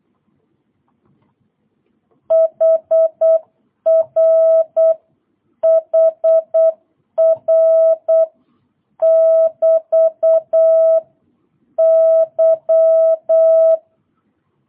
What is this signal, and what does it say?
Morse code, heard as a keyed tone, about 8 wpm, 660 Hz. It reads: HR HR <BT> Y